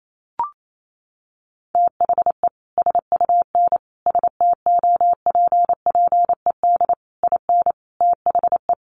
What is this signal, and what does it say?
Morse code, heard as a keyed tone, about 28 wpm, 700 Hz. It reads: THE SUN STOPPED IN THE
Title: A Sound Of Thunder 28wpm